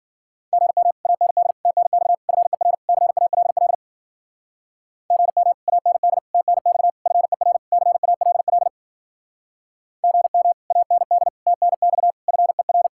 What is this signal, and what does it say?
Morse code, heard as a keyed tone, about 52 wpm, 700 Hz. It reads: GM AND TNX FER CALL GM AND TNX FER CALL GM AND TNX FER